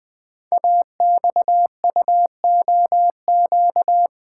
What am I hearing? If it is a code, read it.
Morse code, heard as a keyed tone, about 20 wpm, 700 Hz. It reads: AXUOQ
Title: Code Groups - 5 Letters - 20wpm